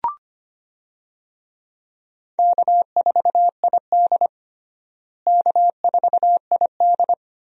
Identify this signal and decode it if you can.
Morse code, heard as a keyed tone, about 25 wpm, 700 Hz. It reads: K4ID K4ID
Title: USA Call Signs - 4 Characters - Repeated 2x - 25wpm